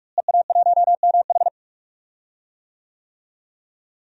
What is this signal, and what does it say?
Morse code, heard as a keyed tone, about 46 wpm, 700 Hz. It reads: EA1MH
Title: DXCC-100 Call Signs - 5 Characters - Repeated 3x - 45wpm